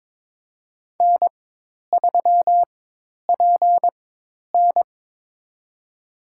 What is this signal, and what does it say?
Morse code, heard as a keyed tone, about 22 wpm, spaced as for 10 wpm, 700 Hz. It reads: N3PN
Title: USA Call Signs - 4 Characters - 4x Character Spacing - 22wpm